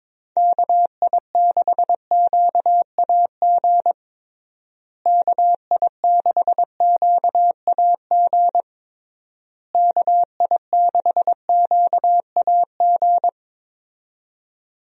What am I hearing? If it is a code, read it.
Morse code, heard as a keyed tone, about 22 wpm, 700 Hz. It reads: KI6QAG KI6QAG KI6QAG
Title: USA Call Signs - 6 Characters - Repeated 3x - 22wpm